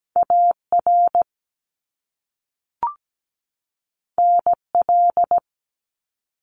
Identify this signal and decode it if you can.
Morse code, heard as a keyed tone, about 17 wpm, 700 Hz. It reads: AR NL
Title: ARRL Sections 17wpm